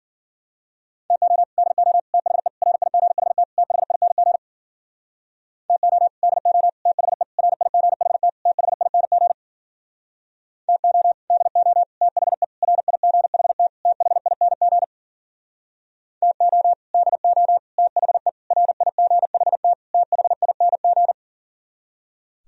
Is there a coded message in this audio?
Yes: Morse code from 1.1 to 21.1 s, heard as a keyed tone, about 52 wpm, 700 Hz. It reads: TO DO THE RIGHT THING TO DO THE RIGHT THING TO DO THE RIGHT THING TO DO THE RIGHT THING